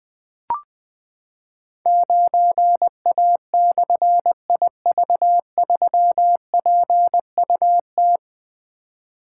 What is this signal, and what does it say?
Morse code, heard as a keyed tone, about 20 wpm, 700 Hz. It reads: 9A/IV3PUT
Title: Difficult Call Signs 20wpm